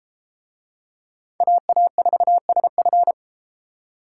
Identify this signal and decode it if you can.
Morse code, heard as a keyed tone, about 30 wpm, 700 Hz. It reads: AA4SF